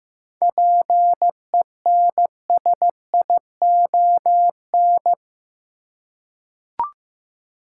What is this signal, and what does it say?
Morse code, heard as a keyed tone, about 15 wpm, 700 Hz. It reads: PENSION
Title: Seven Letter Words - 15wpm